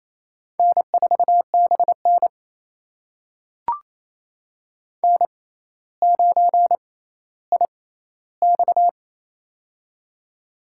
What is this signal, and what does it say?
Morse code, heard as a keyed tone, about 28 wpm, spaced as for 14 wpm, 700 Hz. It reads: N4BN N9IX
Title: USA Call Signs - 4 Characters - 6x Character Spacing - 28wpm